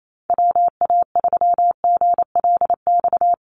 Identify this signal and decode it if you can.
Morse code, heard as a keyed tone, about 28 wpm, 700 Hz. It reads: WA3GLX
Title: USA Call Signs Repeated 2x 28wpm